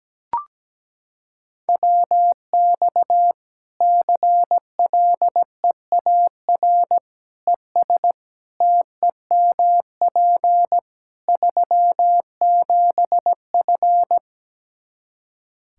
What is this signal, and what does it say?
Morse code, heard as a keyed tone, about 17 wpm, 700 Hz. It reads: WX CLEAR ES TEMP 37F
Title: K1USN SST QSO Practice 17wpm